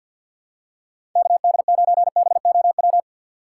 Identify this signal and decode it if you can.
Morse code, heard as a keyed tone, about 48 wpm, 700 Hz. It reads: KD9BOW
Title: USA Call Signs - 6 Characters - 3x Character Spacing - 50wpm